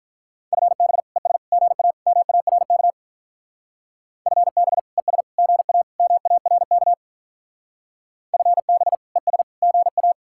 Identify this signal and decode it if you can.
Morse code, heard as a keyed tone, about 45 wpm, 700 Hz. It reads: FB ES GA MARK FB ES GA MARK FB ES GA